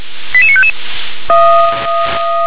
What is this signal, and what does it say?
Five-tone selective call (ZVEI), heard as quick runs of five tones, tones of about 70 ms. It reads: 88955